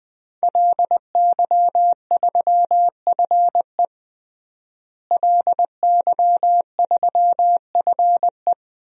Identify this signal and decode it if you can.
Morse code, heard as a keyed tone, about 20 wpm, 700 Hz. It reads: LY3FE LY3FE
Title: DXCC-100 Call Signs - 5 Characters - Repeated 2x - 20wpm